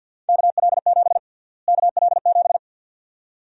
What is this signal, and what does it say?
Morse code, heard as a keyed tone, about 52 wpm, 700 Hz. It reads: KL7 KL7
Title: USA Call Sign Prefixes - Repeated 2x - 50wpm